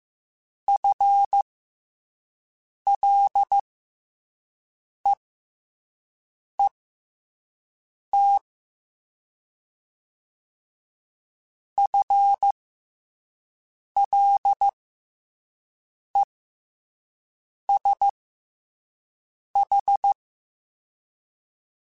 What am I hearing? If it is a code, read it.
Morse code, heard as a keyed tone, about 15 wpm, spaced as for 5 wpm, 800 Hz. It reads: FLEET FLESH
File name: FL Runs / FL Runs mp3s / FL Runs @5